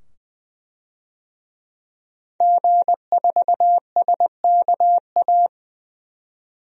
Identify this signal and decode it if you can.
Morse code, heard as a keyed tone, about 20 wpm, 700 Hz. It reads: G4SKA
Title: DXCC-100 Call Signs - 5 Characters - 20wpm